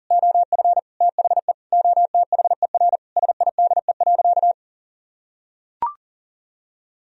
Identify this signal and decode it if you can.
Morse code, heard as a keyed tone, about 40 wpm, 700 Hz. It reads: OF THE OTHER SIDE.
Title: Sentences from Top 1000 Words - Review Part 3 - 40wpm